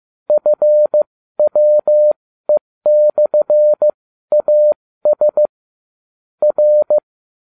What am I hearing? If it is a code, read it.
Morse code, heard as a keyed tone, about 15 wpm, spaced as for 13 wpm, 600 Hz. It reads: FWE/AS R